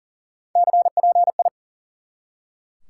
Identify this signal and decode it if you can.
Morse code, heard as a keyed tone, about 40 wpm, 700 Hz. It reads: CPI
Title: Top 90 QSO Elements 40wpm